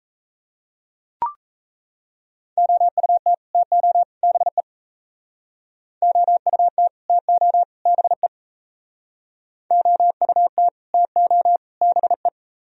Morse code, heard as a keyed tone, about 33 wpm, 700 Hz. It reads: OUT TO BE OUT TO BE OUT TO BE
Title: Speed-Racing - Sets of 3 Words 28wpm